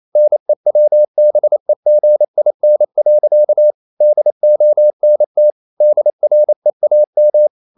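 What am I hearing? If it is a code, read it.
Morse code, heard as a keyed tone, about 28 wpm, 600 Hz. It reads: NEWBEGIN. DONT DREAM
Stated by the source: Morse code practice at 28 wpm, text of Ken Follett's novel Never in English, from CW med Gnister.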